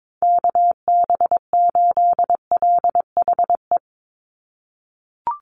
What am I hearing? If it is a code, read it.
Morse code, heard as a keyed tone, about 22 wpm, 700 Hz. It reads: KB8LHE
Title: USA Call Signs 22wpm